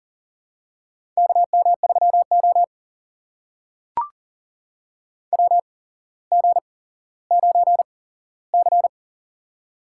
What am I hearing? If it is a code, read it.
Morse code, heard as a keyed tone, about 38 wpm, spaced as for 17 wpm, 700 Hz. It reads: KM3O WG9C
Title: USA Call Signs - 4 Characters - 8x Character Spacing - 40wpm